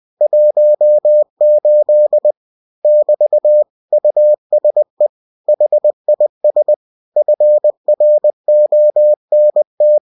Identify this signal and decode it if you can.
Morse code, heard as a keyed tone, about 20 wpm, 600 Hz. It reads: 18 <BT> USE HIS FRONT
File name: cw-books / winnie-the-pooh / winnie-20wpm-page-0018